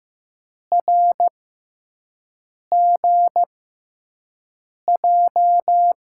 Morse code, heard as a keyed tone, about 16 wpm, spaced as for 5 wpm, 700 Hz. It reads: RGJ
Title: USA Call Signs - 6 Characters - 6x Character Spacing - 15wpm